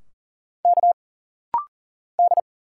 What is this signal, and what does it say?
Morse code, heard as a keyed tone, about 40 wpm, 700 Hz. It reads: K D
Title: Single Letters - Rapid-Fire 40wpm